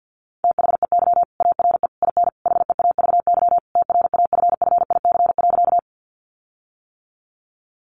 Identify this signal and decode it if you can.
Morse code, heard as a keyed tone, about 50 wpm, 700 Hz. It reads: THEY ARE IN HEAVY TRAFFIC.
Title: Sentences from Top 1300 Words 50wpm